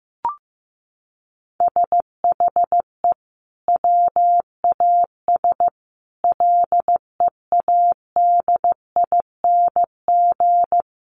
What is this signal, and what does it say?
Morse code, heard as a keyed tone, about 15 wpm, 700 Hz. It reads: SHE WAS LEADING
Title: Sentences from Top 1100 Words 15wpm